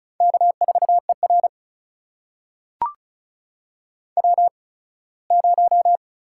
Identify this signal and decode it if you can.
Morse code, heard as a keyed tone, about 36 wpm, spaced as for 21 wpm, 700 Hz. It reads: K4ER W0
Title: USA Call Signs - 8x Character Spacing - 35wpm